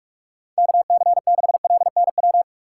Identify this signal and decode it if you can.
Morse code, heard as a keyed tone, about 46 wpm, 700 Hz. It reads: KC6LNW
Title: USA Call Signs - 6 Characters - 4x Character Spacing - 45wpm